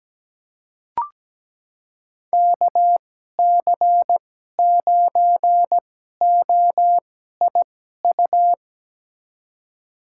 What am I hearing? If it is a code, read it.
Morse code, heard as a keyed tone, about 17 wpm, spaced as for 12 wpm, 700 Hz. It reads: KC9OIU
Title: USA Call Signs - 6 Characters - 2x Character Spacing - 17wpm